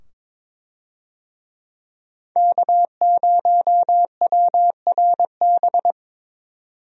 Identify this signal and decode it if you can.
Morse code, heard as a keyed tone, about 22 wpm, 700 Hz. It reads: K0WRB
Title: USA Call Signs - 5 Characters - 22wpm